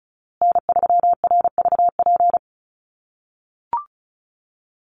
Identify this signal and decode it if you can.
Morse code, heard as a keyed tone, about 34 wpm, 700 Hz. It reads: N3RVP
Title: USA Call Signs Repeated 2x 35wpm